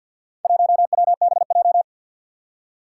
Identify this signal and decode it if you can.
Morse code, heard as a keyed tone, about 51 wpm, 700 Hz. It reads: 1WBJ
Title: Four Challenging Letter-Number Combinations 50wpm